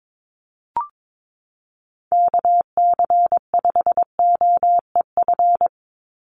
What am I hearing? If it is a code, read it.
Morse code, heard as a keyed tone, about 22 wpm, 700 Hz. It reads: KC5OEF
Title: USA Call Signs 22wpm